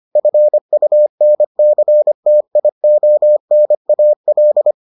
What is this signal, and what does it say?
Morse code, heard as a keyed tone, about 25 wpm, 600 Hz. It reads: FUNCTIONAL